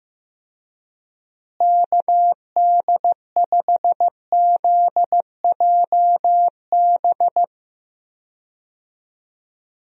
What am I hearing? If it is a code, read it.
Morse code, heard as a keyed tone, about 15 wpm, 700 Hz. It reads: KD5ZJB
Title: USA Call Signs - 6 Characters - Repeated 3x - 15wpm